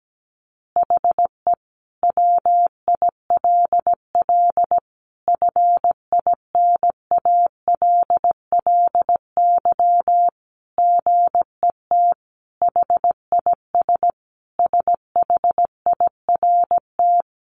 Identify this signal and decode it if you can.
Morse code, heard as a keyed tone, about 17 wpm, 700 Hz. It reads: HE WILL FINALLY GET HIS SHIRT
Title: Additional Easy Phrases 17wpm